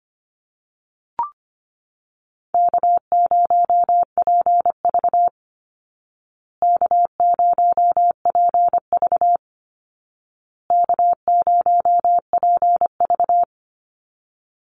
Morse code, heard as a keyed tone, about 25 wpm, 700 Hz. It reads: K0PV K0PV K0PV
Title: Call Signs Repeated 2x - 25wpm